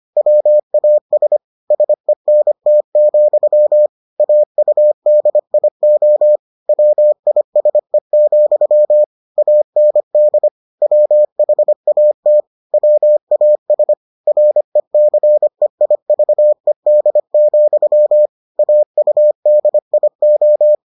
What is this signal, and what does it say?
Morse code, heard as a keyed tone, about 25 wpm, 600 Hz. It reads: WAS SENT, AUDIO WISE, AND WHAT WAS RECEIVED, AUDIO